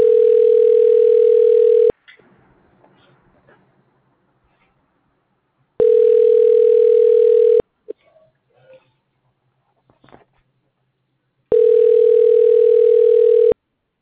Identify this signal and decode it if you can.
Ringback tone (North America), 440 + 480 Hz.